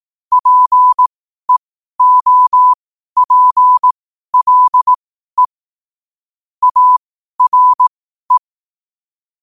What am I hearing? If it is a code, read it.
Morse code, heard as a keyed tone, about 18 wpm, spaced as for 12 wpm, 1000 Hz. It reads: PEOPLE ARE
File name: qotdfiles / 12wpm